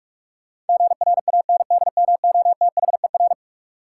Morse code, heard as a keyed tone, about 45 wpm, 700 Hz. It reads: GRANDMOTHER
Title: Sentences from Top 1200 Words - New Word Drill - 45wpm